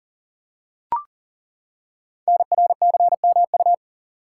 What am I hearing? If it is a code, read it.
Morse code, heard as a keyed tone, about 38 wpm, 690 Hz. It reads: NRCMU
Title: Code Groups - 5 Letters - 40wpm